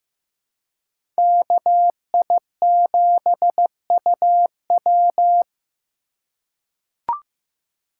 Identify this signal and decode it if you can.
Morse code, heard as a keyed tone, about 15 wpm, 700 Hz. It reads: KI7UW
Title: USA Call Signs - 5 Characters - 4x Character Spacing - 15wpm